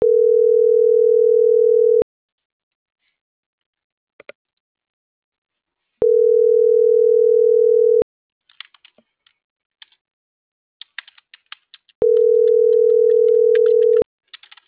Ringback tone (North America), 440 + 480 Hz.